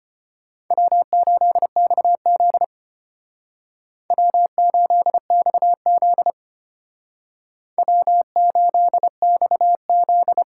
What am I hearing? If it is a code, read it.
Morse code, heard as a keyed tone, about 29 wpm, 700 Hz. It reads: W8XZ W8XZ W8XZ
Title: Speed Racing - USA 1x2 Call Signs 25wpm